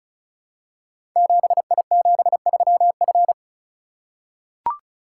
Morse code, heard as a keyed tone, about 36 wpm, 700 Hz. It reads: 7I73F
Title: Code Groups - 5 Letter:Numbers - 35wpm